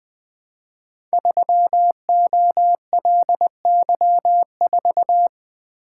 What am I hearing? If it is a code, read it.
Morse code, heard as a keyed tone, about 20 wpm, 700 Hz. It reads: 3OLY4
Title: Code Groups - 5 Letter:Numbers - 20wpm